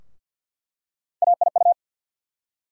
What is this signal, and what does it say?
Morse code, heard as a keyed tone, about 47 wpm, 700 Hz. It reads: AIU
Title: USA Call Sign Postfixes - 3 Letters 50wpm